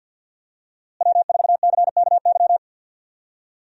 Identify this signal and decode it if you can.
Morse code, heard as a keyed tone, about 50 wpm, 700 Hz. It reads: W4CKY
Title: USA Call Signs - 5 Characters - Repeated 3x - 50wpm